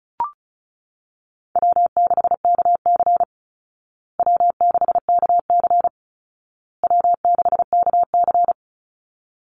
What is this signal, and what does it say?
Morse code, heard as a keyed tone, about 34 wpm, 700 Hz. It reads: W6KC W6KC W6KC
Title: Easy Call Signs Repeated 3x - 35wpm